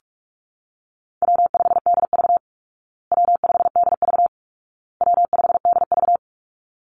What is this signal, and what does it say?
Morse code, heard as a keyed tone, about 44 wpm, 700 Hz. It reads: W5DV W5DV W5DV
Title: Easy Call Signs Repeated 3x - 45wpm